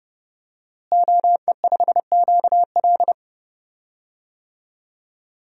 Morse code, heard as a keyed tone, about 29 wpm, 700 Hz. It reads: OE5QL
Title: DXCC-100 Call Signs - 5 Characters - Repeated 2x - 30wpm